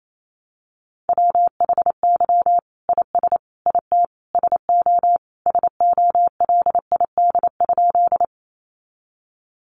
Morse code, heard as a keyed tone, about 28 wpm, 700 Hz. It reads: WHY IS IT SO SOLID?
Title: Sentences from Top 1500 Words 28wpm